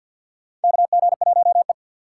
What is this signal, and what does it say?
Morse code, heard as a keyed tone, about 51 wpm, 700 Hz. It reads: KG1E
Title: USA Call Signs - 4 Characters - 3x Character Spacing - 50wpm